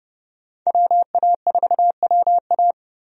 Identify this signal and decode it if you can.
Morse code, heard as a keyed tone, about 30 wpm, 700 Hz. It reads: WA4WA
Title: USA Call Signs - 5 Characters - 30wpm